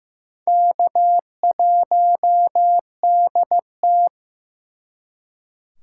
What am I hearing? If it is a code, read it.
Morse code, heard as a keyed tone, about 18 wpm, 700 Hz. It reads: K1DT
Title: Speed Racing - USA 1x2 Call Signs 15wpm